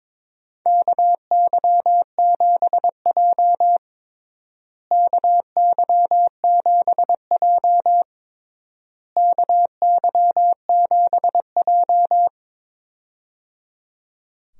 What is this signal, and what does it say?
Morse code, heard as a keyed tone, about 22 wpm, 700 Hz. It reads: KY7J KY7J KY7J
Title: USA Call Signs - 4 Characters - Repeated 3x - 22wpm